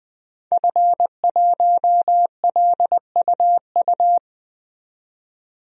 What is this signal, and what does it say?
Morse code, heard as a keyed tone, about 21 wpm, 700 Hz. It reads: F1LUU
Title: DXCC-100 Call Signs - 5 Characters - 20wpm